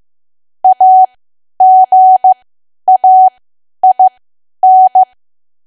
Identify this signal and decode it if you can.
Morse code, heard as a keyed tone, about 15 wpm, spaced as for 10 wpm, 750 Hz. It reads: AGAIN
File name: Archive / 10 WPM / 10WPM